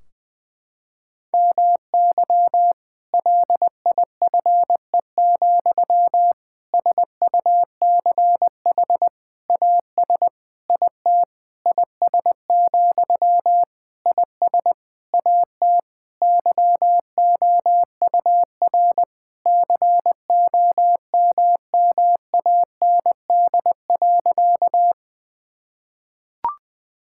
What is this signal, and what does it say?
Morse code, heard as a keyed tone, about 21 wpm, 700 Hz. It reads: MY LIFE, SUCH AS IT IS, IS AT YOUR COMMAND.
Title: Sentences from Top 1600 Words 20wpm